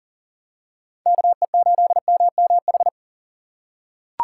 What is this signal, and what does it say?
Morse code, heard as a keyed tone, about 40 wpm, 700 Hz. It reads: KE8MMH
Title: USA Call Signs - 6 Characters - 6x Character Spacing - 40wpm